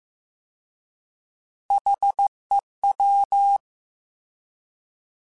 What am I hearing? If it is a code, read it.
Morse code, heard as a keyed tone, about 15 wpm, 800 Hz. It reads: HEW
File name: Alphabetical / Alphabetical mp3s / Hs @13